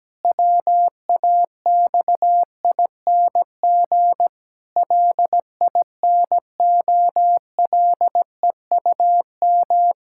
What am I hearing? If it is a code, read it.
Morse code, heard as a keyed tone, about 17 wpm, 700 Hz. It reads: WAXING LINOLEUM
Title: Pangram Challenge v2 - 17wpm